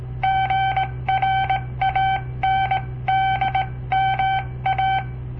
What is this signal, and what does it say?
Morse code, heard as a keyed tone, about 17 wpm, 770 Hz. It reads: GRANDMA
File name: papertapes / local / Code Practice-01